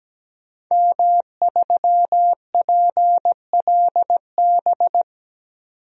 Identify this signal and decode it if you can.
Morse code, heard as a keyed tone, about 17 wpm, 700 Hz. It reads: M3PLB
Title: DXCC-100 Call Signs - 5 Characters - 17wpm